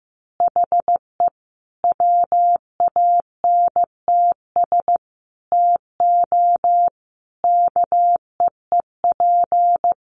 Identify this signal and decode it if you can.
Morse code, heard as a keyed tone, about 15 wpm, 700 Hz. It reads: HE WANTS TO KEEP
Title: Additional Easy Phrases 15wpm